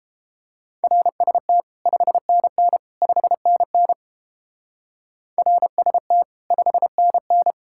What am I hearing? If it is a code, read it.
Morse code, heard as a keyed tone, about 30 wpm, 700 Hz. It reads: RST 5NN 5NN RST 5NN